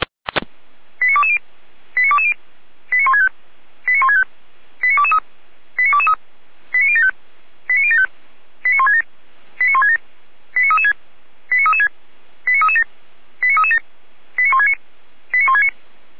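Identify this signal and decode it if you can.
Five-tone selective call (ZVEI), heard as quick runs of five tones, tones of about 70 ms. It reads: 89220 89220 89165 89165 89223 89223 89075 89075 89167 89167 89336 89336 89337 89337 89169 89169
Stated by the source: Five-tone ZVEI selective call received on an SDR.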